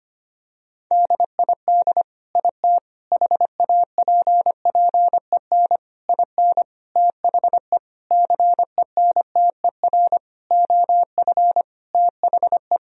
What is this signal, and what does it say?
Morse code, heard as a keyed tone, about 25 wpm, 700 Hz. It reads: DID IT HAPPEN IN THE CENTER OF THE
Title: Sentences from Top 1000 Words - Review Part 2 - 25wpm